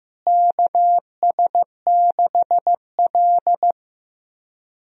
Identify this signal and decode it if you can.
Morse code, heard as a keyed tone, about 15 wpm, 700 Hz. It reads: KS6L
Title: USA Call Signs - 4 Characters - Repeated 2x - 15wpm